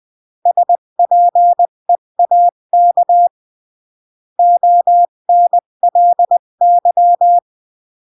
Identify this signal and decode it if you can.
Morse code, heard as a keyed tone, about 20 wpm, 700 Hz. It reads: SPEAK ONLY
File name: CW / jodorowsky / 82 commandments 20-15